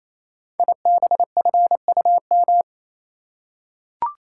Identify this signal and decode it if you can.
Morse code, heard as a keyed tone, about 27 wpm, 700 Hz. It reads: IBFUM
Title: Code Groups - 5 Letter:Numbers - 28wpm